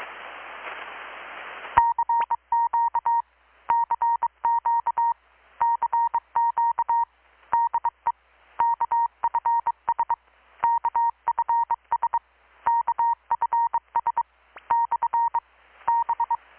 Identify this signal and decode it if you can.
Morse code, heard as a keyed tone, about 24 wpm, 930 Hz. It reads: CQ CQ CQ DE KFS KFS KFS / B
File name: blog / swl / 6-abril-2019-12695-USB-KPS